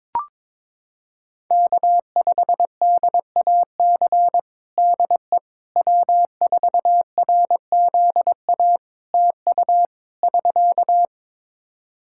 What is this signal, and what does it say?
Morse code, heard as a keyed tone, about 22 wpm, 700 Hz. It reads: K5DAC DE W4RZA TU <SK>
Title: Practice QSO - Exchange 4 - Ending - 22wpm